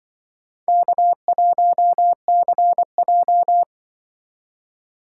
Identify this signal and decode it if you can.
Morse code, heard as a keyed tone, about 25 wpm, 700 Hz. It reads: K1CJ
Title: Speed Racing - USA Call Signs 20wpm